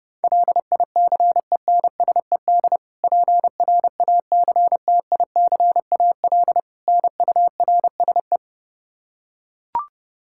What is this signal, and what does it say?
Morse code, heard as a keyed tone, about 29 wpm, 700 Hz. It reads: LICENSED PRACTICAL NURSE
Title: Practice QSO - Exchange 3 - More Info - 30wpm